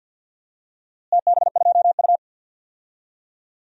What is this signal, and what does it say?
Morse code, heard as a keyed tone, about 52 wpm, 700 Hz. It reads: TB2U
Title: DXCC-100 Call Signs - 4 Characters - Repeated 3x - 50wpm